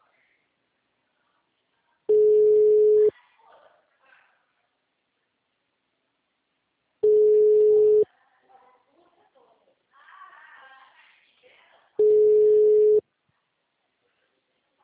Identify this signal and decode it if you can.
Ringback tone (Europe, 425 Hz), with speech.